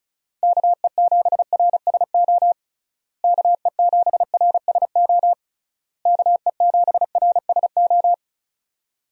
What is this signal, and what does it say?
Morse code, heard as a keyed tone, about 34 wpm, 700 Hz. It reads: KE7RSO KE7RSO KE7RSO
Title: USA Call Signs - 6 Characters - Repeated 3x - 35wpm